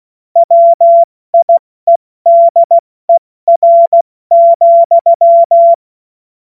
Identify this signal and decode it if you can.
Morse code, heard as a keyed tone, about 16 wpm, 680 Hz. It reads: WIEDER,
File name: Morsekurs / Witze / Witz 10.Oktober 2023 70 F80